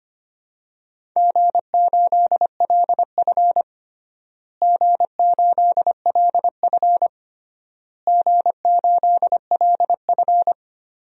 Morse code, heard as a keyed tone, about 25 wpm, 700 Hz. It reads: G8LF G8LF G8LF
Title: DXCC-100 Call Signs - 4 Characters - Repeated 3x - 25wpm